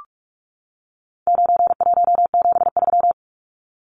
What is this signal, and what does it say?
Morse code, heard as a keyed tone, about 44 wpm, 700 Hz. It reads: 9173
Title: Numbers - 4-Digit 45wpm